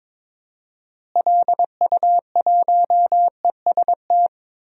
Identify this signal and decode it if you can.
Morse code, heard as a keyed tone, about 22 wpm, 700 Hz. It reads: LU1EST
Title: DXCC-100 Call Signs - 6 Characters - Repeated 3x - 22wpm